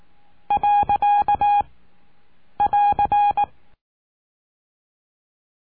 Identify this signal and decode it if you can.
Morse code, heard as a keyed tone, about 18 wpm, 820 Hz. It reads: . <AR>
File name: cursos / CURSO DE TELEGRAFIA / Telegrafia - Trilha 20